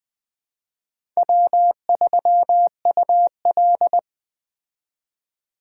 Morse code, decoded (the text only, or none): W3UL